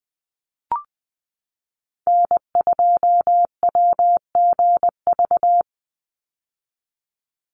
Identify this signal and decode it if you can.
Morse code, heard as a keyed tone, about 20 wpm, 700 Hz. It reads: N2WGV
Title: USA Call Signs 20wpm